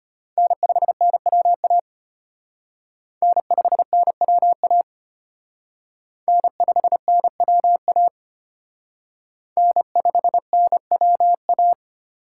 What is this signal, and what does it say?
Morse code, heard as a keyed tone, about 34 wpm, 700 Hz. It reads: N5NWA N5NWA N5NWA N5NWA